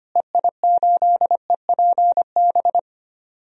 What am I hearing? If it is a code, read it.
Morse code, heard as a keyed tone, about 25 wpm, 700 Hz. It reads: EI8EPB
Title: DXCC-100 Call Signs - 6 Characters - Repeated 2x - 25wpm